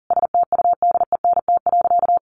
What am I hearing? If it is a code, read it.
Morse code, heard as a keyed tone, about 40 wpm, 700 Hz. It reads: STUDENT.
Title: Additional Easy Phrases 40wpm